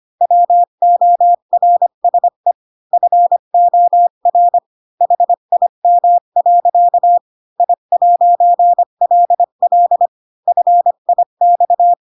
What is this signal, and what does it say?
Morse code, heard as a keyed tone, about 25 wpm, 700 Hz. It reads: WORSE FOR HIM. I'LL FIX
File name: Pinocchio / Pinocchio ch1 20 25 ik1hge